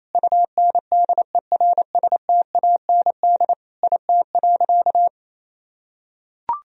Morse code, heard as a keyed tone, about 27 wpm, 700 Hz. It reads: UNDERSTAND IT.
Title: Sentences from Top 1000 Words - Review Part 2 - 28wpm